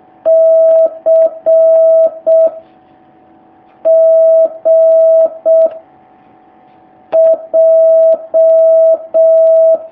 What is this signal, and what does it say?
Morse code, heard as a keyed tone, about 6 wpm, 650 Hz. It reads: C G J
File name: jh8cbh / 2019cw